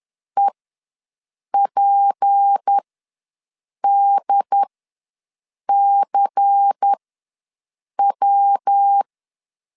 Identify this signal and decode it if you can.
Morse code, heard as a keyed tone, about 10 wpm, spaced as for 6 wpm, 800 Hz. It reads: EPDCW
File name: LEZIONI / 6 PKX0 / 6 PKX ws 10wpm 3706